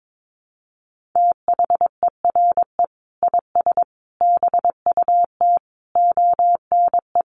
Morse code, heard as a keyed tone, about 22 wpm, 700 Hz. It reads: THERE IS BUT ONE
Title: Sentences from Top 100 Words 22wpm